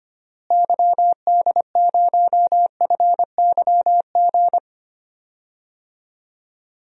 Morse code, heard as a keyed tone, about 25 wpm, 700 Hz. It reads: YD0FYG
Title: DXCC-100 Call Signs - 6 Characters - Repeated 2x - 25wpm